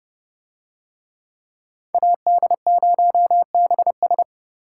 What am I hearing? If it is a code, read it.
Morse code, heard as a keyed tone, about 30 wpm, 700 Hz. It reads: AD0BS